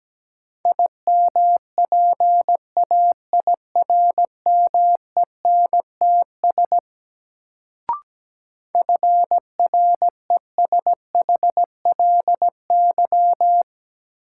Morse code, heard as a keyed tone, about 17 wpm, 700 Hz. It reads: IMPAIRMENTS FRESHLY
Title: Sets of 5 Words Encore - New Word Drill - 17wpm